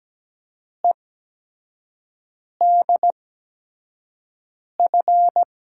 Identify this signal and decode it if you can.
Morse code, heard as a keyed tone, about 17 wpm, spaced as for 5 wpm, 700 Hz. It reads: EDF
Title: USA Call Signs - 6 Characters - 8x Character Spacing - 17wpm